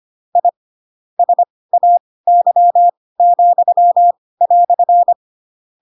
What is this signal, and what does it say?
Morse code, heard as a keyed tone, about 25 wpm, spaced as for 18 wpm, 700 Hz. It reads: I SAY,"
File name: Pinocchio / Pinocchio ch7 12 25 ik1hge